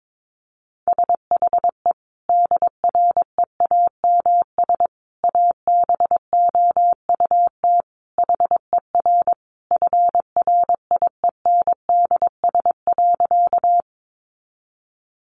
Morse code, heard as a keyed tone, about 22 wpm, 700 Hz. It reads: SHE DREAMS ABOUT HER FRIENDS.
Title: Easy Phrases 22wpm